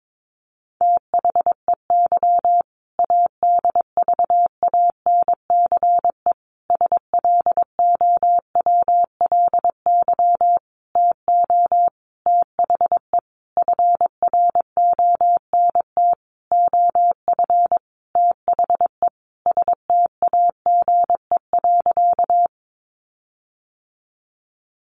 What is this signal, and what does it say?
Morse code, heard as a keyed tone, about 22 wpm, 700 Hz. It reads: THEY ADVANCE SLOWLY TO THE FRONT OF THE STAGE.
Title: Sentences from Top 1400 Words 22wpm